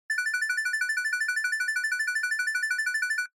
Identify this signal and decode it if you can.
There is no code.